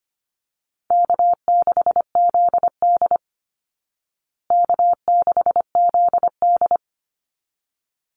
Morse code, heard as a keyed tone, about 25 wpm, 700 Hz. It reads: K6ZD K6ZD
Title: Easy Call Signs Repeated 2x - 25wpm